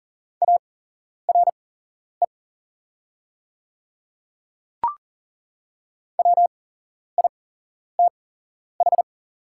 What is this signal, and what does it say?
Morse code, heard as a keyed tone, about 40 wpm, spaced as for 11 wpm, 700 Hz. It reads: ARE WITH
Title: 8x Character Spacing - Top 100 Words - 40wpm